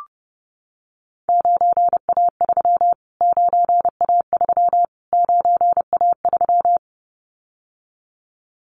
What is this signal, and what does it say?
Morse code, heard as a keyed tone, about 30 wpm, 690 Hz. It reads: 9A3 9A3 9A3
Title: Callsign Prefixes for Top 25 DXCC - Repeated 3x - 30wpm